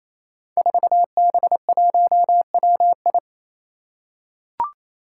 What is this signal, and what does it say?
Morse code, heard as a keyed tone, about 28 wpm, 700 Hz. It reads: 4B1WI
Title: Five Challenging Letter-Number Combinations 28wpm